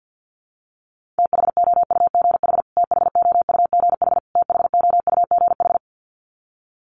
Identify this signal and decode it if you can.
Morse code, heard as a keyed tone, about 50 wpm, 700 Hz. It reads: THOUGH THOUGH THOUGH
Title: Top 500 Words Repeated 50wpm